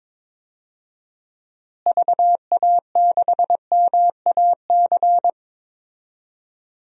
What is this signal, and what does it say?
Morse code, heard as a keyed tone, about 22 wpm, 700 Hz. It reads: VA6MAC
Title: DXCC-100 Call Signs - 6 Characters - Repeated 3x - 22wpm